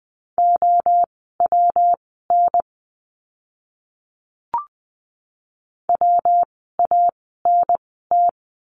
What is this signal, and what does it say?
Morse code, heard as a keyed tone, about 20 wpm, spaced as for 15 wpm, 700 Hz. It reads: OWN WANT
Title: 237 Lesson 38f - Letter q - Words 20wpm